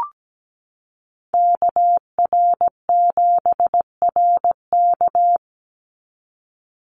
Morse code, heard as a keyed tone, about 17 wpm, 700 Hz. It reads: KR7RK
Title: NA SOTA Activator Callsigns 17wpm